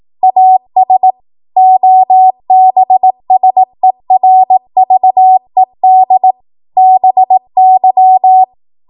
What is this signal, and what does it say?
Morse code, heard as a keyed tone, about 18 wpm, 750 Hz. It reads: AS OBSERVED BY
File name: Morse / 18WPM